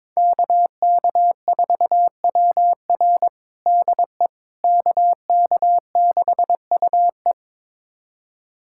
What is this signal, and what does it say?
Morse code, heard as a keyed tone, about 22 wpm, 700 Hz. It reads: KK4WR DE KK6UE
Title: USA Callsign DE Callsign - 5 Characters - 22wpm